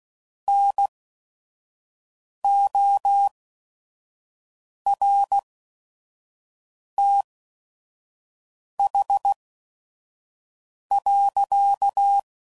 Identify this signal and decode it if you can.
Morse code, heard as a keyed tone, about 16 wpm, spaced as for 5 wpm, 800 Hz. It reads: NORTH.
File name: CW Code Groups @5WPM / Weather Sets / Weather 4 28